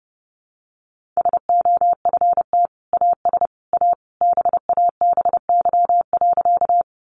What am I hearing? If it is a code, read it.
Morse code, heard as a keyed tone, about 28 wpm, 700 Hz. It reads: SOFT AS A BABY.
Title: Sentences from Top 1500 Words 30wpm